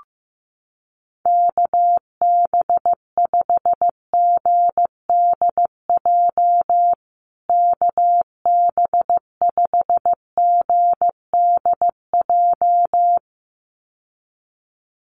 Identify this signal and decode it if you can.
Morse code, heard as a keyed tone, about 15 wpm, 700 Hz. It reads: KB5GDJ KB5GDJ
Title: USA Call Signs Repeated 2x 15wpm